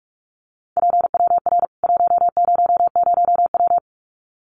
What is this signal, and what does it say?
Morse code, heard as a keyed tone, about 44 wpm, 700 Hz. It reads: PWR 100W
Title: Common QSO Phrases 45wpm